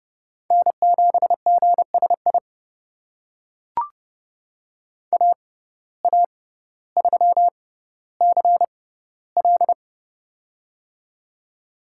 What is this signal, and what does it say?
Morse code, heard as a keyed tone, about 30 wpm, spaced as for 15 wpm, 700 Hz. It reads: N7GSI AA3CL
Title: USA Call Signs - 5 Characters - 6x Character Spacing - 30wpm